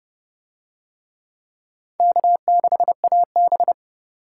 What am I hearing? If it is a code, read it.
Morse code, heard as a keyed tone, about 29 wpm, 700 Hz. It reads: K6AB